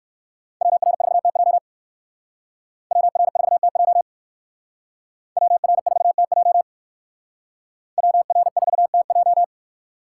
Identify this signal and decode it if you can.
Morse code, heard as a keyed tone, about 60 wpm, 700 Hz. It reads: WR4TJ WR4TJ WR4TJ WR4TJ